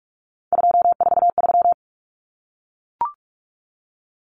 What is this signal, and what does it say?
Morse code, heard as a keyed tone, about 44 wpm, 700 Hz. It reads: 243
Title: Common Cut Numbers - 3-Digit 45wpm